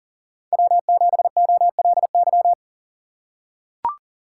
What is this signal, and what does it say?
Morse code, heard as a keyed tone, about 39 wpm, 700 Hz. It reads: W7OLY
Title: USA Call Signs - 5 Characters - 3x Character Spacing - 40wpm